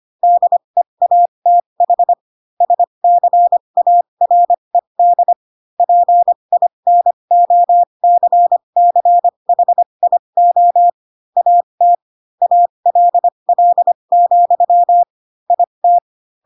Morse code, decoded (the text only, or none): DEATH SCARED PINOCCHIO AT ALL, IT